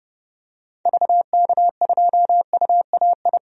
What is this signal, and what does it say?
Morse code, heard as a keyed tone, about 30 wpm, 700 Hz. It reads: VK2UAI
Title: DXCC-100 Call Signs - 6 Characters - 30wpm